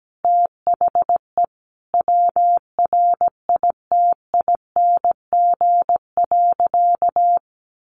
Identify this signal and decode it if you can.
Morse code, heard as a keyed tone, about 17 wpm, 700 Hz. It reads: THE WRITING.
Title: Sentences from Top 1400 Words 17wpm